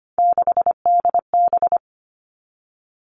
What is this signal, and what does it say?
Morse code, heard as a keyed tone, about 25 wpm, 700 Hz. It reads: 6DB
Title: Three Challenging Letter-Number Combinations 25wpm